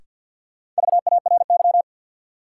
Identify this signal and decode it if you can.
Morse code, heard as a keyed tone, about 53 wpm, 700 Hz. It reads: VARY
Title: Sentences from Top 1200 Words - New Word Drill Alt - 50wpm